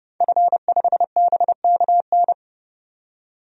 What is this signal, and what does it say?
Morse code, heard as a keyed tone, about 30 wpm, 700 Hz. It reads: F5BKN
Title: DXCC-100 Call Signs - 5 Characters - 30wpm